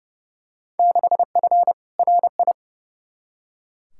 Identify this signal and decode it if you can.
Morse code, heard as a keyed tone, about 30 wpm, 700 Hz. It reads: 6F RI
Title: Field Day Exchanges Encore - 30wpm